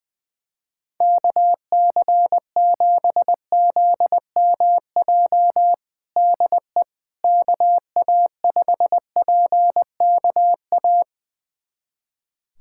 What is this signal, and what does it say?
Morse code, heard as a keyed tone, about 20 wpm, 700 Hz. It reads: KC7ZMJ DE KA5PKA
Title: USA Callsign DE Callsign - 6 Characters - 20wpm